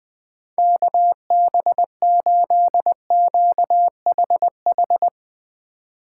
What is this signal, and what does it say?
Morse code, heard as a keyed tone, about 19 wpm, 700 Hz. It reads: KB8QHH
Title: Speed Racing - USA 2x3 Call Signs 17wpm